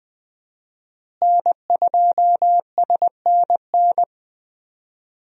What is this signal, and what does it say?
Morse code, heard as a keyed tone, about 20 wpm, 700 Hz. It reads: N2SNN
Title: USA Call Signs - 5 Characters - Repeated 3x - 20wpm